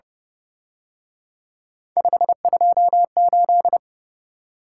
Morse code, heard as a keyed tone, about 29 wpm, 700 Hz. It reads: 528